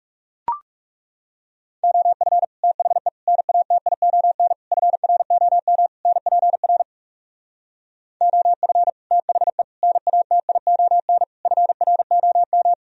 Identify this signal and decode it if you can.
Morse code, heard as a keyed tone, about 40 wpm, 700 Hz. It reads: OF THE NATION FROM NPR OF THE NATION FROM